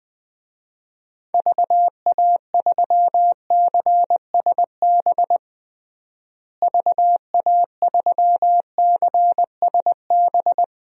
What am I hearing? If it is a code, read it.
Morse code, heard as a keyed tone, about 20 wpm, 700 Hz. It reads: VA3CSB VA3CSB
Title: DXCC-100 Call Signs - 6 Characters - Repeated 3x - 20wpm